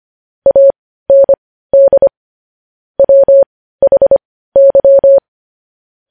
Morse code, heard as a keyed tone, about 25 wpm, spaced as for 15 wpm, 550 Hz. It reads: AND WHY
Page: news headlines in morse code at 15 wpm